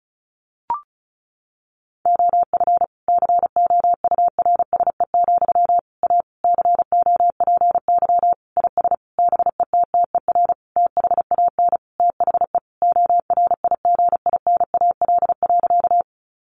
Morse code, heard as a keyed tone, about 34 wpm, 700 Hz. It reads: OF COURSE, A COPY IS BETTER THAN THE ORIGINAL.
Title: Sentences from Top 1400 Words 35wpm